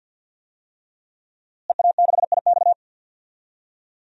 Morse code, heard as a keyed tone, about 52 wpm, 700 Hz. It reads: EA6IX
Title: DXCC-100 Call Signs - 5 Characters - Repeated 3x - 50wpm